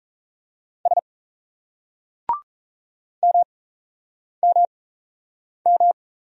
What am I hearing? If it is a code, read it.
Morse code, heard as a keyed tone, about 38 wpm, spaced as for 8 wpm, 700 Hz. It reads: S MMM